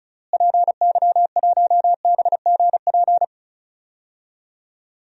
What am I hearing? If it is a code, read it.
Morse code, heard as a keyed tone, about 36 wpm, 700 Hz. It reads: PY1BGP
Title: DXCC-100 Call Signs - 6 Characters - Repeated 3x - 35wpm